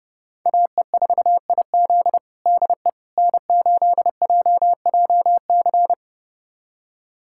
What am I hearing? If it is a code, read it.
Morse code, heard as a keyed tone, about 30 wpm, 700 Hz. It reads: AE4IZ DE N8JJC
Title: USA Callsign DE Callsign - 5 Characters - 30wpm